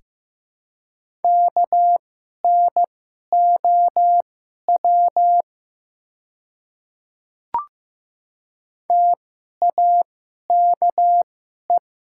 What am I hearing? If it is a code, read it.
Morse code, heard as a keyed tone, about 15 wpm, spaced as for 11 wpm, 700 Hz. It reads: KNOW TAKE
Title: 2x Character Spacing - Top 500 Words - 15wpm